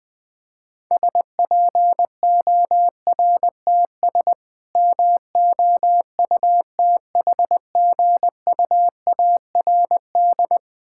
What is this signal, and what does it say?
Morse code, heard as a keyed tone, about 20 wpm, 700 Hz. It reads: SPORTS MOUTHGUARD
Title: Difficult Words - Speed-Racing with Hints - 20wpm